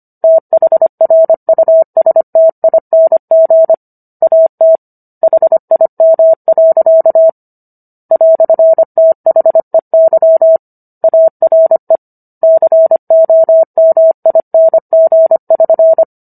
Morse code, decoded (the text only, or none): THRUSTING AT HIM. "THEY ARE COMING<SN>